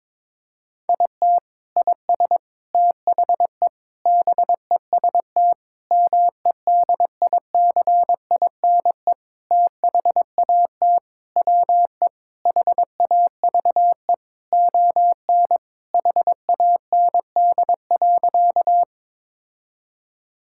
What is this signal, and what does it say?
Morse code, heard as a keyed tone, about 22 wpm, 700 Hz. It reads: IT IS THE BEST MEDICINE THAT WE HAVE ON HAND.
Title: Sentences from Top 1000 Words - Review Part 6 - 22wpm